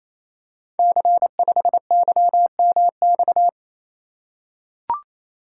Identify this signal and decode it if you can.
Morse code, heard as a keyed tone, about 29 wpm, 690 Hz. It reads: C5YMX
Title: Code Groups - 5 Letter:Numbers - 28wpm